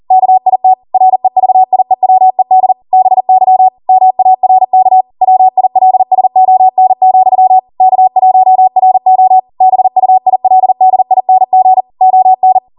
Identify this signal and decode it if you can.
Morse code, heard as a keyed tone, about 40 wpm, 750 Hz. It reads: KIT REVIEWED BY MARK WILSON, K1RO BUILDING ON
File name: Morse / 40WPM